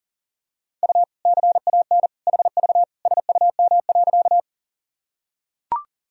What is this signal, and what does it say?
Morse code, heard as a keyed tone, about 40 wpm, 700 Hz. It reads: U CAN HV SUM.
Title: Sentences from Top 100 Words and Abbreviations 40wpm